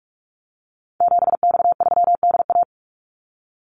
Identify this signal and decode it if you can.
Morse code, heard as a keyed tone, about 44 wpm, 700 Hz. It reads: 7X3DA
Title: Call Signs Repeated 3x - 45wpm